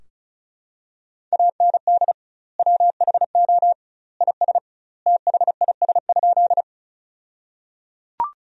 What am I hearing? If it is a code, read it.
Morse code, heard as a keyed tone, about 34 wpm, 700 Hz. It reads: AND WHO IS THIS?
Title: 2x Word-Spacing - Sentences from Top 100 Words - 35wpm